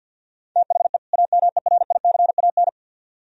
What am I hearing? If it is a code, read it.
Morse code, heard as a keyed tone, about 63 wpm, 700 Hz. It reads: THE AMERICAN